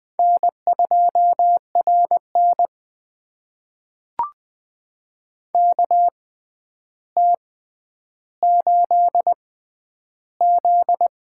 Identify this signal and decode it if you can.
Morse code, heard as a keyed tone, about 20 wpm, spaced as for 10 wpm, 700 Hz. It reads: N2RN KT8Z
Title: USA Call Signs - 4 Characters - 6x Character Spacing - 20wpm